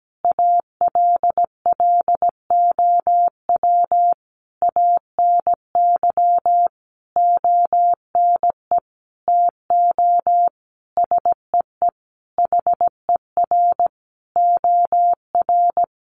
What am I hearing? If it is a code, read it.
Morse code, heard as a keyed tone, about 17 wpm, 700 Hz. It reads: ALLOW ANY ONE TO SEE HER OR
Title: Sentences from Top 1200 Words 17wpm